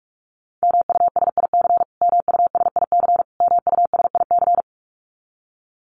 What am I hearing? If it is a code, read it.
Morse code, heard as a keyed tone, about 46 wpm, 700 Hz. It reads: MUSIC MUSIC MUSIC
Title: Top 300 Words Repeated 45wpm